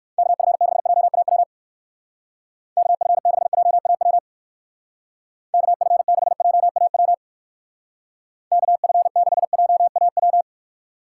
Morse code, heard as a keyed tone, about 56 wpm, 700 Hz. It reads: KF6JAW KF6JAW KF6JAW KF6JAW
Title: Speed Racing - USA 2x3 Call Signs 45wpm